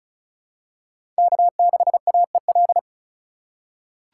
Morse code, heard as a keyed tone, about 36 wpm, 700 Hz. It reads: K6AEL
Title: USA Call Signs - 4x Character Spacing - 35wpm